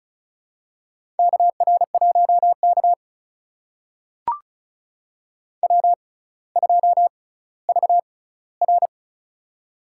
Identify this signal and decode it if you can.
Morse code, heard as a keyed tone, about 36 wpm, spaced as for 18 wpm, 700 Hz. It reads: KR1K W2VR
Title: USA Call Signs - 4 Characters - 6x Character Spacing - 35wpm